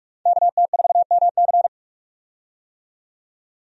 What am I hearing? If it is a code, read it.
Morse code, heard as a keyed tone, about 46 wpm, 700 Hz. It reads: KT4MC
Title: USA Call Signs - 5 Characters - 45wpm